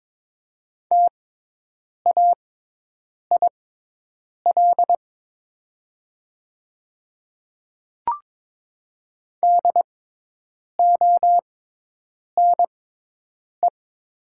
Morse code, heard as a keyed tone, about 23 wpm, spaced as for 8 wpm, 700 Hz. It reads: TAIL DONE
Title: 6x Character Spacing - Top 500 Words - 22wpm